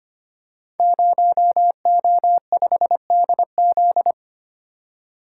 Morse code, heard as a keyed tone, about 25 wpm, 700 Hz. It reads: 0O5DZ
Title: Code Groups - 5 Letter:Numbers - 25wpm